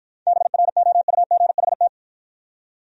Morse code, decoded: BROUGHT

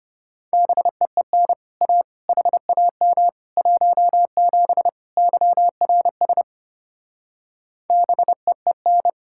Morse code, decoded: BEEN A HAM 17 YRS BEEN

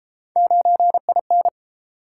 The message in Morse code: 9IN

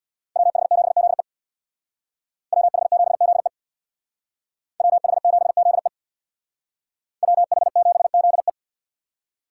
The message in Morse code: WH7ZE WH7ZE WH7ZE WH7ZE

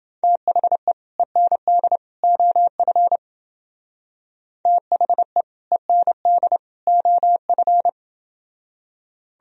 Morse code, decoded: THE END OF THE END OF

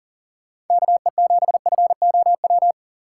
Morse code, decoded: KE7FOW

